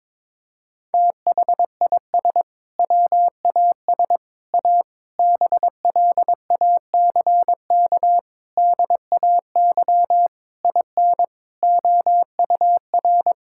Morse code, heard as a keyed tone, about 21 wpm, 700 Hz. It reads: THIS WAS A BLACK DAY IN OUR
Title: Sentences from Top 1600 Words 22wpm